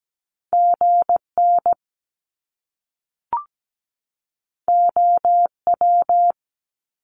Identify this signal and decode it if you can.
Morse code, heard as a keyed tone, about 17 wpm, 700 Hz. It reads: GN OW
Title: Top Two Letter Combinations in Words 17wpm